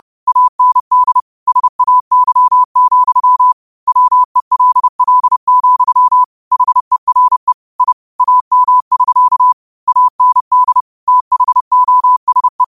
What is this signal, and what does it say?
Morse code, heard as a keyed tone, about 30 wpm, 1000 Hz. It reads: AND SAY, WELL, HERE I AM<SK> AND THOSE
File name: qotdfiles / 30wpm